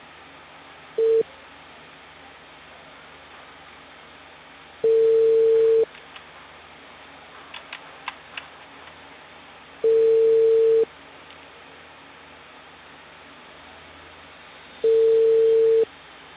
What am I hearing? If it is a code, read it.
Ringback tone (China, 450 Hz).